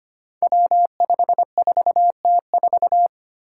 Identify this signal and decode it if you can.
Morse code, heard as a keyed tone, about 25 wpm, 700 Hz. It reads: W54T4
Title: Five Challenging Letter-Number Combinations 25wpm